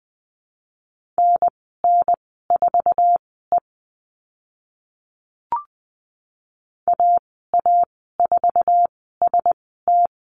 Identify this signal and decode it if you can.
Morse code, heard as a keyed tone, about 20 wpm, spaced as for 14 wpm, 700 Hz. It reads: NN4E AA4ST
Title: 062 Lesson 11g - Letter h - Call Signs 20wpm